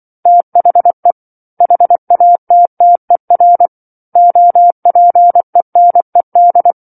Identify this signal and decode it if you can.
Morse code, heard as a keyed tone, about 24 wpm, 700 Hz. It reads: THE HATTER OPENED
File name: 07-Chapter-07 24 WPM